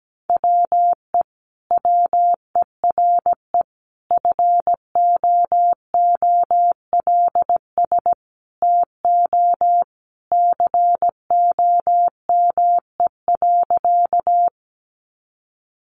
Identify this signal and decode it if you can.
Morse code, heard as a keyed tone, about 17 wpm, 700 Hz. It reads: WE WERE FOOLS TO COME.
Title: A Sound Of Thunder 17wpm